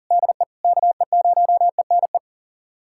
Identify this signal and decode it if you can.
Morse code, heard as a keyed tone, about 40 wpm, 700 Hz. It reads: DE KE0ENE
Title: USA Callsign DE Callsign 40wpm